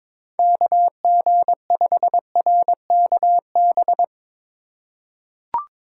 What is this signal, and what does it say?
Morse code, heard as a keyed tone, about 22 wpm, 700 Hz. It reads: KG5RKB
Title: USA Call Signs - 6 Characters - 2x Character Spacing - 22wpm